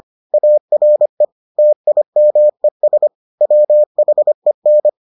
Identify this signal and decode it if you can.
Morse code, heard as a keyed tone, about 25 wpm, 600 Hz. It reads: ARE TIMES WHEN